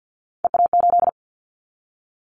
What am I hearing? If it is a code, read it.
Morse code, heard as a keyed tone, about 50 wpm, 700 Hz. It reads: EA8
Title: Callsign Prefixes for Top 100 DXCC - 50wpm